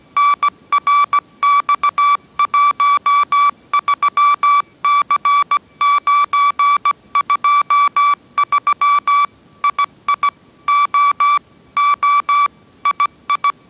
Morse code, heard as a keyed tone, about 17 wpm, 1150 Hz. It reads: NRX13C923 II O O II
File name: 18wpm